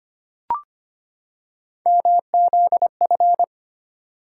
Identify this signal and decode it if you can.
Morse code, heard as a keyed tone, about 25 wpm, 700 Hz. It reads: MZF
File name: Three Letter-Number Combinations - 25wpm